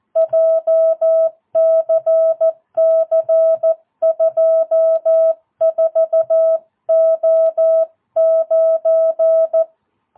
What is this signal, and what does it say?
Morse code, heard as a keyed tone, about 14 wpm, 660 Hz. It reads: JCC 24O9
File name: jh8cbh / 2022cw